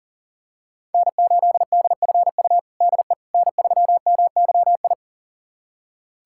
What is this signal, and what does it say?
Morse code, heard as a keyed tone, about 40 wpm, 700 Hz. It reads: N8DFU DE N3MYI
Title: USA Callsign DE Callsign - 5 Characters - 40wpm